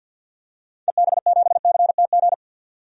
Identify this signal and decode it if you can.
Morse code, heard as a keyed tone, about 52 wpm, 700 Hz. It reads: EB7CTG
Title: DXCC-100 Call Signs - 6 Characters - 50wpm